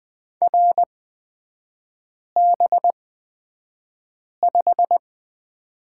Morse code, heard as a keyed tone, about 20 wpm, spaced as for 5 wpm, 700 Hz. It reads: RB5
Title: 255 Lesson 41e - Period - Three Characters 20wpm